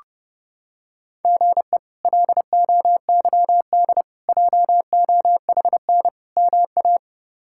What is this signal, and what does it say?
Morse code, heard as a keyed tone, about 28 wpm, 700 Hz. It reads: GE LOYD JOHN MA
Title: K1USN SST QSO Practice - 30wpm